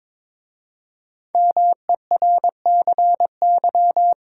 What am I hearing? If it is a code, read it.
Morse code, heard as a keyed tone, about 22 wpm, 700 Hz. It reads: MERCY
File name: 5 Letter Words - First 3 Letters Given - 22wpm